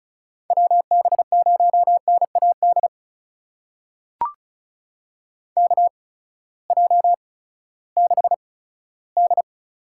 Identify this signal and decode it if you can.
Morse code, heard as a keyed tone, about 36 wpm, spaced as for 18 wpm, 700 Hz. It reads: WB0NAD KJ6D
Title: USA Call Signs - 6 Characters - 8x Character Spacing - 35wpm